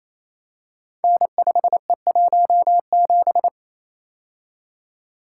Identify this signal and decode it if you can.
Morse code, heard as a keyed tone, about 27 wpm, 700 Hz. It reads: N5E17
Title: Code Groups - 5 Letter:Numbers - 28wpm